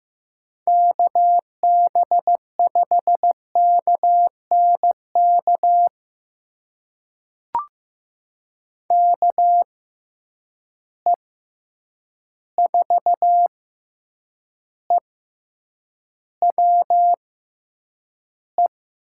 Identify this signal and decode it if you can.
Morse code, heard as a keyed tone, about 15 wpm, spaced as for 8 wpm, 700 Hz. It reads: KB5KNK KE4EWE
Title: USA Call Signs - 6 Characters - 6x Character Spacing - 15wpm